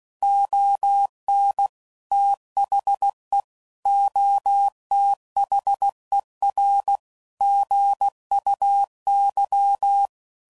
Morse code, heard as a keyed tone, about 16 wpm, 800 Hz. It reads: ON THE OTHER GUY